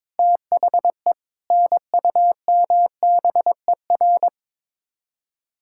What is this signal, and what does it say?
Morse code, heard as a keyed tone, about 22 wpm, 700 Hz. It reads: THE NUMBER
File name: practice / 22wpm / Build A Phrase - 5-Word Phrases - 22wpm